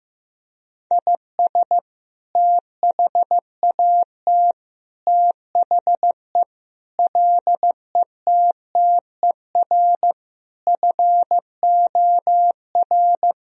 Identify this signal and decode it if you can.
Morse code, heard as a keyed tone, about 15 wpm, 700 Hz. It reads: IS THAT THE LETTER FOR
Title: Sentences from Top 1000 Words - Review Part 2 - 15wpm